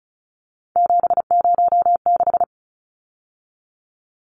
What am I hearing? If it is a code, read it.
Morse code, heard as a keyed tone, about 34 wpm, 700 Hz. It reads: 706
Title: Numbers - 3-Digit 35wpm